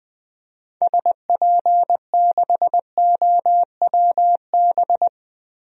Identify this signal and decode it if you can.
Morse code, heard as a keyed tone, about 21 wpm, 700 Hz. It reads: SP6OWB
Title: DXCC-100 Call Signs - 6 Characters - Repeated 2x - 20wpm